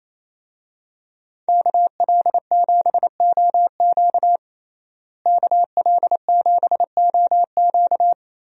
Morse code, heard as a keyed tone, about 28 wpm, 700 Hz. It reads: KL7OQ KL7OQ
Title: DXCC-100 Call Signs - 5 Characters - Repeated 3x - 28wpm